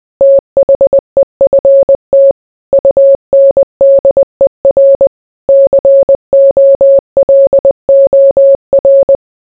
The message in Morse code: THEFT UNDER COLOR

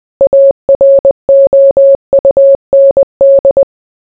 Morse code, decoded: AROUND

